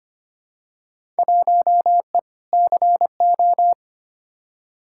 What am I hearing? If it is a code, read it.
Morse code, heard as a keyed tone, about 25 wpm, 700 Hz. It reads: 1E CO
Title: Field Day Exchanges - 25wpm